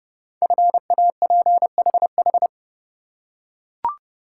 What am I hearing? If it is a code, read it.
Morse code, heard as a keyed tone, about 30 wpm, 700 Hz. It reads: FAPHH
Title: Code Groups - 5 Letters - 30wpm